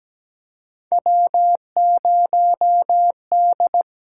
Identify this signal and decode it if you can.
Morse code, heard as a keyed tone, about 17 wpm, 700 Hz. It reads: W0D
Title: USA Call Signs - 4 Characters - 8x Character Spacing - 17wpm